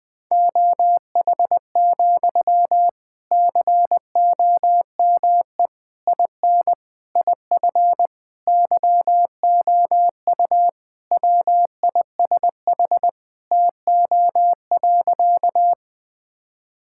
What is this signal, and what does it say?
Morse code, heard as a keyed tone, about 21 wpm, 700 Hz. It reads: OH, COME IN IF YOU WISH TO.
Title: Sentences from Top 1600 Words 20wpm